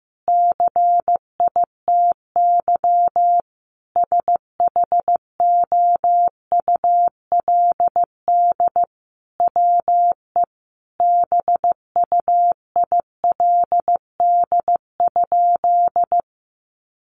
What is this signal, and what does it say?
Morse code, heard as a keyed tone, about 15 wpm, 700 Hz. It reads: CITY SHOULD WE BUILD?
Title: Sentences from Top 1300 Words 15wpm